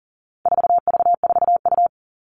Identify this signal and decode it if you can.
Morse code, heard as a keyed tone, about 37 wpm, 700 Hz. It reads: 4V4U
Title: All Cut Numbers - 4-Digit 40wpm